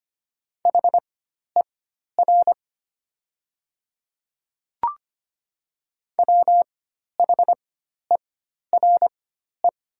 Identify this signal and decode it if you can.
Morse code, heard as a keyed tone, about 25 wpm, spaced as for 12 wpm, 700 Hz. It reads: HER WHERE
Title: 4x Character Spacing - Top 200 Words - 25wpm